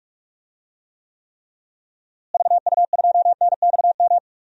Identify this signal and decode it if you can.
Morse code, heard as a keyed tone, about 41 wpm, 700 Hz. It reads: VU2NXM
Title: DXCC-100 Call Signs - 6 Characters - Repeated 3x - 45wpm